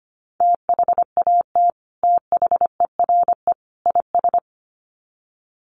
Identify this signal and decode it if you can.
Morse code, heard as a keyed tone, about 25 wpm, 700 Hz. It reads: THAT THERE IS
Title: Sets of 3 Words 25wpm